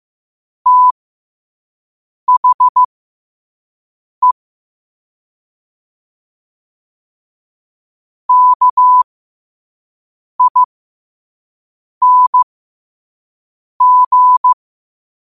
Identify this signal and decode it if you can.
Morse code, heard as a keyed tone, about 15 wpm, spaced as for 5 wpm, 1000 Hz. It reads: THE KING
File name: qotdfiles / 5wpm